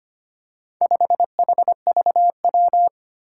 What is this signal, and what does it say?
Morse code, heard as a keyed tone, about 25 wpm, 700 Hz. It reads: 5HVW
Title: Four Challenging Letter-Number Combinations 25wpm